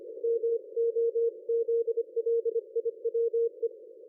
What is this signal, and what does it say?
Morse code, heard as a keyed tone, about 25 wpm, 460 Hz. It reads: MOZLIWE
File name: dlazaawansowanych / nagraniazpasma / ExpertSDR2 line out 23.01.2020 14.04.01